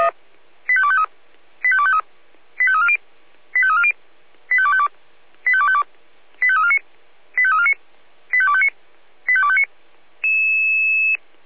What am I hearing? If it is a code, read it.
Five-tone selective call (ZVEI), heard as quick runs of five tones, tones of about 70 ms. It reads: 85253 85253 85340 85340 85252 85252 85349 85349 85259 85259
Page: SDR/ZVEI